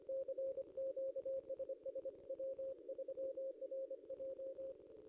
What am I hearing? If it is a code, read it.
Morse code, heard as a keyed tone, about 26 wpm, 540 Hz. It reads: CQ SS W3RJ
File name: contests / ss-cw-k5tr-op-n5rz -0900